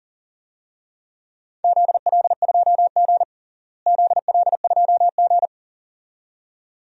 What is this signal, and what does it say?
Morse code, heard as a keyed tone, about 40 wpm, 700 Hz. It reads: ZL2G ZL2G
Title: DXCC-100 Call Signs - 4 Characters - Repeated 2x - 40wpm